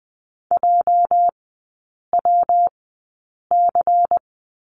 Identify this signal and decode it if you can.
Morse code, heard as a keyed tone, about 20 wpm, 700 Hz. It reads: J W C
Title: 205 Lesson 33e - Letter j - Three Characters 20wpm